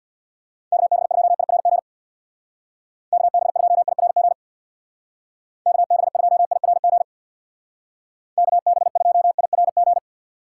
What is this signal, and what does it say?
Morse code, heard as a keyed tone, about 64 wpm, 700 Hz. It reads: KB2IRG KB2IRG KB2IRG KB2IRG